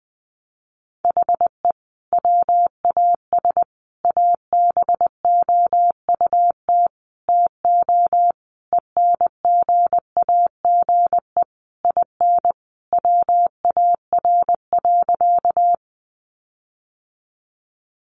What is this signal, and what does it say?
Morse code, heard as a keyed tone, about 20 wpm, 700 Hz. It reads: HE WAS ABOUT TO ENGAGE IN WAR.
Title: Sentences from Top 1500 Words 20wpm